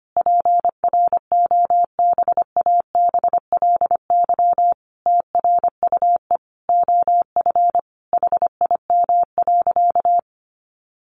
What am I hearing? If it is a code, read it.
Morse code, heard as a keyed tone, about 25 wpm, 700 Hz. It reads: PROBABLY TRUE OF HIM.
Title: Sentences from Top 700 Words 25wpm